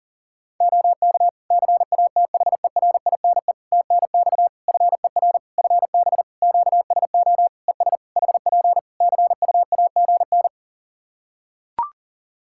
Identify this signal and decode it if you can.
Morse code, heard as a keyed tone, about 40 wpm, 700 Hz. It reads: OK CATHERINE TNX FER FB QSO ES HP CUAGN
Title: Practice QSO - Exchange 4 - Ending - 40wpm